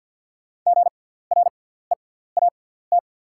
Morse code, heard as a keyed tone, about 48 wpm, spaced as for 17 wpm, 700 Hz. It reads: GREAT
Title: 6x Character Spacing - Top 300 Words - 50wpm